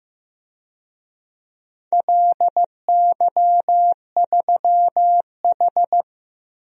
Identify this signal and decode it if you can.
Morse code, heard as a keyed tone, about 15 wpm, 700 Hz. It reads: LY3H
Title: DXCC-100 Call Signs - 4 Characters - 15wpm